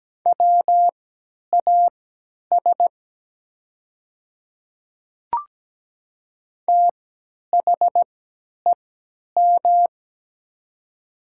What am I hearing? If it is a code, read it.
Morse code, heard as a keyed tone, about 17 wpm, spaced as for 10 wpm, 700 Hz. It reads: WAS THEM
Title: 3x Character Spacing - Top 200 Words - 17wpm